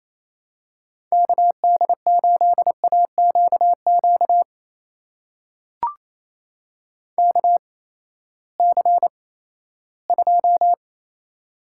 Morse code, heard as a keyed tone, about 28 wpm, spaced as for 16 wpm, 700 Hz. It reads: KD8AQQ KC2
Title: USA Call Signs - 6 Characters - 8x Character Spacing - 28wpm